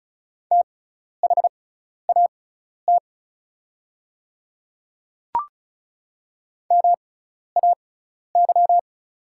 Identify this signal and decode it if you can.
Morse code, heard as a keyed tone, about 36 wpm, spaced as for 12 wpm, 700 Hz. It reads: THAT MAY
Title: 6x Character Spacing - Top 100 Words - 35wpm